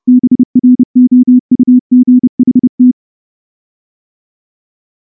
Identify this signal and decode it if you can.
Morse code, heard as a keyed tone, about 29 wpm, 250 Hz. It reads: BROUGHT
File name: morse / morse-training-30WPM